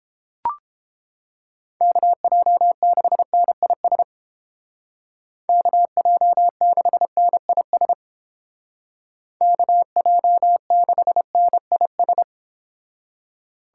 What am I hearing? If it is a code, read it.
Morse code, heard as a keyed tone, about 29 wpm, 700 Hz. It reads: KJ6NIS KJ6NIS KJ6NIS